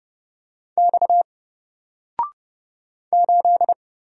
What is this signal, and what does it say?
Morse code, heard as a keyed tone, about 30 wpm, 700 Hz. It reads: X 8
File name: Single-Letter-Number-30wpm